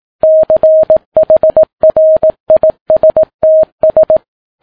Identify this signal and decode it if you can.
Morse code, heard as a keyed tone, about 18 wpm, 630 Hz. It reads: CHRISTS
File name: Ham Radio Sat